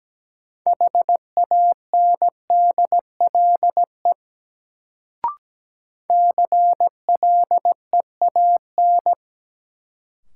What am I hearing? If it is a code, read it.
Morse code, heard as a keyed tone, about 17 wpm, 700 Hz. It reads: HANDLE CLEAN
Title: Sentences from Top 1100 Words - New Word Drill Alt - 17wpm